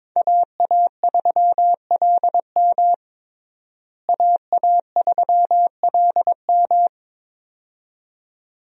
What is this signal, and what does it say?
Morse code, heard as a keyed tone, about 22 wpm, 700 Hz. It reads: AA3LM AA3LM
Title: USA Call Signs - 5 Characters - Repeated 2x - 22wpm